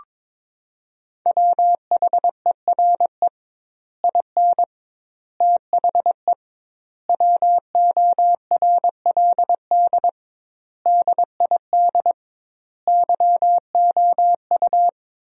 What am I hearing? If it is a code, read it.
Morse code, heard as a keyed tone, about 21 wpm, 700 Hz. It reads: WHERE IN THE WORLD DID YOU
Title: 2x Word-Spacing - Sentences from Top 200 Words - 22wpm